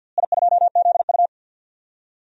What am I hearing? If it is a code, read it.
Morse code, heard as a keyed tone, about 49 wpm, 700 Hz. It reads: I2ZU
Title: DXCC-100 Call Signs - 4 Characters - Repeated 3x - 50wpm